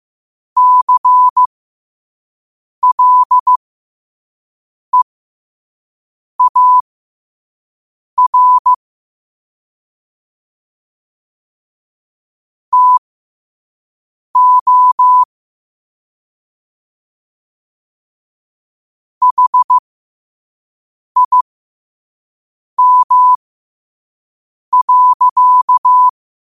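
Morse code, heard as a keyed tone, about 15 wpm, spaced as for 5 wpm, 1000 Hz. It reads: CLEAR TO HIM.
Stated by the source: Morse code at 5 words per minute.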